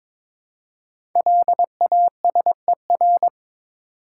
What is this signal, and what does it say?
Morse code, heard as a keyed tone, about 22 wpm, 700 Hz. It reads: LASER
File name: 5 Letter Words - First 3 Letters Given - 22wpm